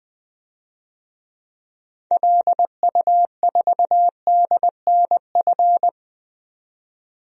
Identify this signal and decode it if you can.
Morse code, heard as a keyed tone, about 20 wpm, 700 Hz. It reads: LU4DNF
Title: DXCC-100 Call Signs - 6 Characters - 20wpm